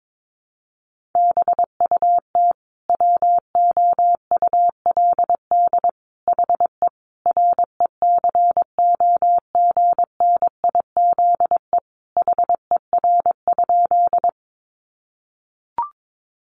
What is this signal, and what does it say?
Morse code, heard as a keyed tone, about 22 wpm, 700 Hz. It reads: BUT WOULD HE RECOGNIZE HER?
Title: Sentences from Top 1400 Words 22wpm